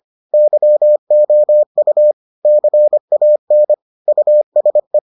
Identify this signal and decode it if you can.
Morse code, heard as a keyed tone, about 25 wpm, 600 Hz. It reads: YOU CAN USE